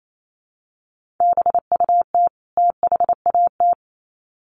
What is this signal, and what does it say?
Morse code, heard as a keyed tone, about 28 wpm, 700 Hz. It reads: BUT THAT
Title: Sets of 2 Words 28wpm